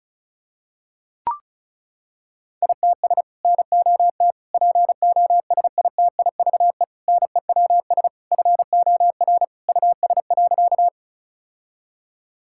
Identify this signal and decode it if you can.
Morse code, heard as a keyed tone, about 36 wpm, 700 Hz. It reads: ITS NOT POSITIVE NEWS FOR US.
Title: Sentences from Top 1000 Words - Review Part 5 - 35wpm